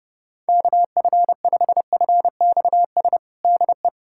Morse code, heard as a keyed tone, about 32 wpm, 700 Hz. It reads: KF5FXS DE
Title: USA Callsign DE Callsign - 6 Characters - 30wpm